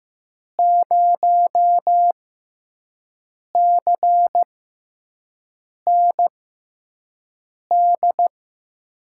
Morse code, heard as a keyed tone, about 15 wpm, spaced as for 5 wpm, 700 Hz. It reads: 0CND
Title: USA Call Signs - 6 Characters - 6x Character Spacing - 15wpm